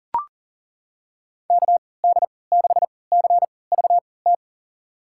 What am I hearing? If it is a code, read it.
Morse code, heard as a keyed tone, about 40 wpm, spaced as for 23 wpm, 700 Hz. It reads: KD6CVT
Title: USA Call Signs - 6 Characters - 3x Character Spacing - 40wpm